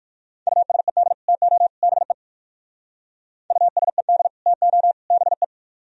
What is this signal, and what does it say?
Morse code, heard as a keyed tone, about 62 wpm, 700 Hz. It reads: USED TO BE USED TO BE